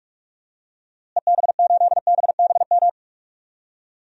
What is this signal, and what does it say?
Morse code, heard as a keyed tone, about 46 wpm, 690 Hz. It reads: EB8BBM
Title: DXCC-100 Call Signs - 6 Characters - Repeated 2x - 45wpm